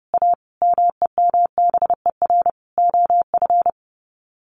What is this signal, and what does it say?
Morse code, heard as a keyed tone, about 30 wpm, 700 Hz. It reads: A MEMBER OF
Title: Sets of 5 Words Encore2 30wpm